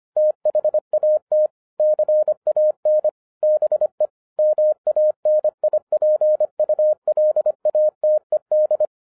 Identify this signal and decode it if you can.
Morse code, heard as a keyed tone, about 25 wpm, 610 Hz. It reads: THAT CAN BE MANIPULATED